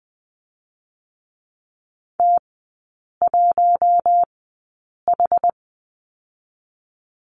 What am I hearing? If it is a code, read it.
Morse code, heard as a keyed tone, about 20 wpm, spaced as for 8 wpm, 690 Hz. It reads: T1H
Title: 060 Lesson 11e - Letter h - Three Characters 20wpm